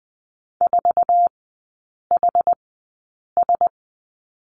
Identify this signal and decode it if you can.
Morse code, heard as a keyed tone, about 20 wpm, spaced as for 8 wpm, 700 Hz. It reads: 4HS
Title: 067 Lesson 12e - Letter d - Three Characters 20wpm